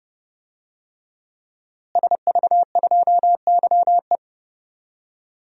Morse code, heard as a keyed tone, about 30 wpm, 700 Hz. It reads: SV2YE
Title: DXCC-100 Call Signs - 5 Characters - Repeated 3x - 30wpm